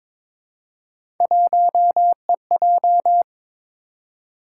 Morse code, heard as a keyed tone, about 23 wpm, 700 Hz. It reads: 1EJ
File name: practice / 22wpm / Three Letter-Number Combinations - 22wpm